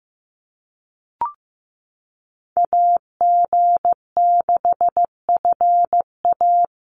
Morse code, heard as a keyed tone, about 15 wpm, 700 Hz. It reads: AG6FA
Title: USA Call Signs Repeated 3x 15wpm